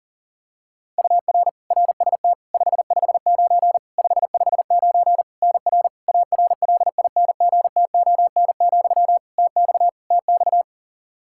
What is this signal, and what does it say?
Morse code, heard as a keyed tone, about 40 wpm, 700 Hz. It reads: UR RST 559 559 NR ARLINGTON, TX TX
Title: Practice QSO - Exchange 1 - Beginning - 40wpm